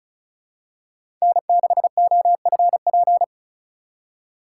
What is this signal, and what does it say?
Morse code, heard as a keyed tone, about 36 wpm, 700 Hz. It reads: N6OFP
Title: USA Call Signs - 5 Characters - 3x Character Spacing - 35wpm